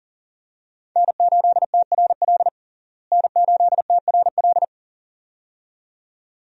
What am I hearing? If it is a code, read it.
Morse code, heard as a keyed tone, about 37 wpm, 700 Hz. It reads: N8TRL N8TRL
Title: USA Call Signs - 5 Characters - Repeated 2x - 40wpm